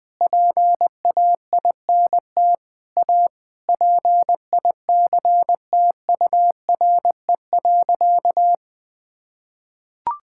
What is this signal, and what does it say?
Morse code, heard as a keyed tone, about 20 wpm, 700 Hz. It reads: PAINT A PICTURE.
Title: Sentences with 8-Word Phrases - 20wpm